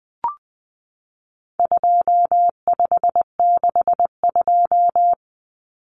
Morse code, heard as a keyed tone, about 20 wpm, 700 Hz. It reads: 2562
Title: Common Cut Numbers - 4-Digit 20wpm